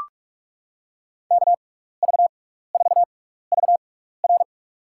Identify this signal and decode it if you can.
Morse code, heard as a keyed tone, about 43 wpm, spaced as for 15 wpm, 700 Hz. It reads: KV4VR
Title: USA Call Signs - 5 Characters - 6x Character Spacing - 45wpm